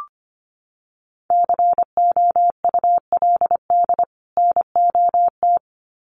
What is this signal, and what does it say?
Morse code, heard as a keyed tone, about 25 wpm, 700 Hz. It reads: COULD NOT
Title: Sets of 2 Words 25wpm